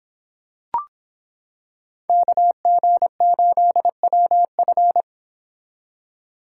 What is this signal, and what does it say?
Morse code, heard as a keyed tone, about 26 wpm, 700 Hz. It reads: KG8WF